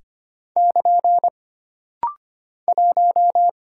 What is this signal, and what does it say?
Morse code, heard as a keyed tone, about 25 wpm, 700 Hz. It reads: <KN> 1
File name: Single Character (Letter:Number:Punctuation:Prosign) - Rapid-Fire - 25wpm